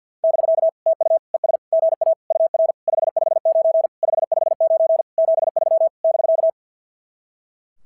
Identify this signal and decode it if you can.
Morse code, heard as a keyed tone, about 46 wpm, 660 Hz. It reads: <BK> TU ES GA UR 559 559 73 <BK>
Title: SOTA QSO Practice 50wpm